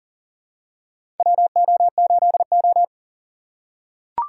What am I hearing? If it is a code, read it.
Morse code, heard as a keyed tone, about 37 wpm, 700 Hz. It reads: WO8O
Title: USA Call Signs - 4 Characters - 6x Character Spacing - 40wpm